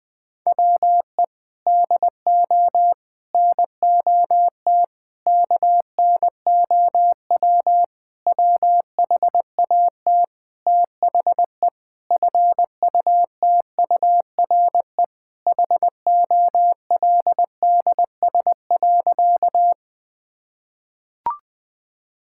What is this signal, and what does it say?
Morse code, heard as a keyed tone, about 20 wpm, 700 Hz. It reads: WE DO NOT KNOW WHAT THE FUTURE HOLDS.
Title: Sentences with 5-Word Phrases - 20wpm